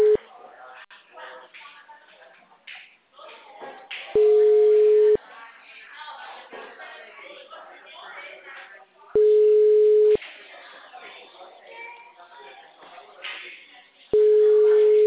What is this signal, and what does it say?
Ringback tone (Europe, 425 Hz), with speech.